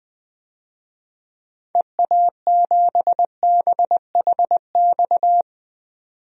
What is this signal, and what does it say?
Morse code, heard as a keyed tone, about 20 wpm, 700 Hz. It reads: EA7BHX
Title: DXCC-100 Call Signs - 6 Characters - 20wpm